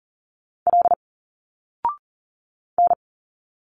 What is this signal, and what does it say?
Morse code, heard as a keyed tone, about 40 wpm, 700 Hz. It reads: L N
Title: Single Letters 40wpm